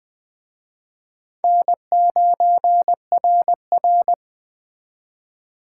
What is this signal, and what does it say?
Morse code, heard as a keyed tone, about 21 wpm, 700 Hz. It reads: N9RR